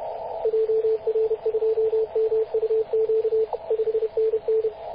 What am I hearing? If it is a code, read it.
Morse code, heard as a keyed tone, about 31 wpm, 440 Hz, with radio noise. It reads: JR2MUQ 5NN
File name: posts / 2016-mar-19-0241z-vk9ck-21-032mhz-cw